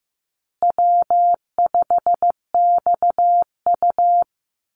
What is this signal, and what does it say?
Morse code, heard as a keyed tone, about 15 wpm, 700 Hz. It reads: W5XU
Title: Call Signs Repeated 2x - 15wpm